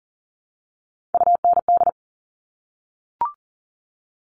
Morse code, heard as a keyed tone, about 40 wpm, 700 Hz. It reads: UND
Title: All Cut Numbers - 3-Digit 40wpm